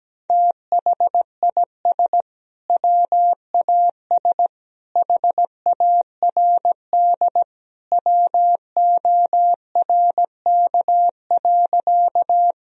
Morse code, heard as a keyed tone, about 17 wpm, 700 Hz. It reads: THIS WAS HARD WORK.
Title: Sentences from Top 1000 Words - Review Part 1 - 17wpm